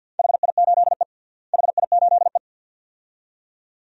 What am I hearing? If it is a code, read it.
Morse code, heard as a keyed tone, about 50 wpm, 700 Hz. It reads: HI8E HI8E
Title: DXCC-100 Call Signs - 4 Characters - Repeated 2x - 50wpm